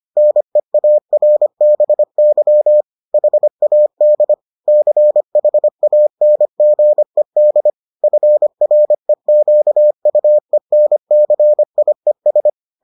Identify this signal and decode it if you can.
Morse code, heard as a keyed tone, about 25 wpm, 600 Hz. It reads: NEARBY HAD CHANGED FREQUENCIES